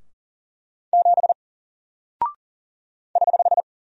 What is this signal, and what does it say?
Morse code, heard as a keyed tone, about 40 wpm, 700 Hz. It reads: 7 <HH>
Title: Single Character (Letter:Number:Punctuation:Prosign) Alt 40wpm